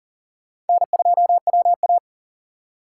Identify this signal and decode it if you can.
Morse code, heard as a keyed tone, about 51 wpm, 700 Hz. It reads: N2WA